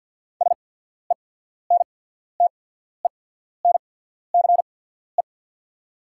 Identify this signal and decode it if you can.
Morse code, heard as a keyed tone, about 48 wpm, spaced as for 14 wpm, 690 Hz. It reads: SENTENCE
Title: 8x Character Spacing - Top 300 Words - 50wpm